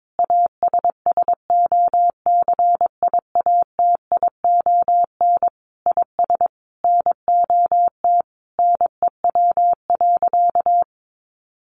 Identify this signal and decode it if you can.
Morse code, heard as a keyed tone, about 22 wpm, 700 Hz. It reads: ASSOCIATION IS NOT NEW.
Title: Sentences from Top 1100 Words 22wpm